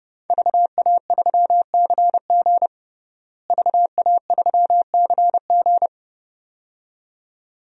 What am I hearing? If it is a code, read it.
Morse code, heard as a keyed tone, about 30 wpm, 700 Hz. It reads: VA3CG VA3CG
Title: DXCC-100 Call Signs - 5 Characters - Repeated 2x - 30wpm